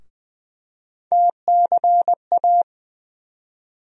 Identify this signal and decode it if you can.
Morse code, heard as a keyed tone, about 20 wpm, 700 Hz. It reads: TCA
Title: USA Call Sign Postfixes - 3 Letters 20wpm